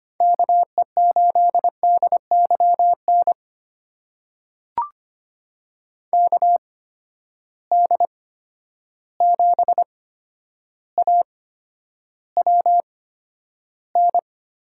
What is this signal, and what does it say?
Morse code, heard as a keyed tone, about 25 wpm, spaced as for 11 wpm, 700 Hz. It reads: KE8DYN KD7AWN
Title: USA Call Signs - 6 Characters - 8x Character Spacing - 25wpm